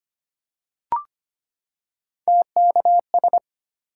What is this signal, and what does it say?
Morse code, heard as a keyed tone, about 25 wpm, 700 Hz. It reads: TKS
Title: Top 180 QSO Elements 25wpm